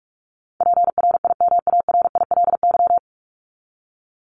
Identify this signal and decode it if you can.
Morse code, heard as a keyed tone, about 44 wpm, 700 Hz. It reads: PRIMARILY
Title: Top 1001-1250 Words 45wpm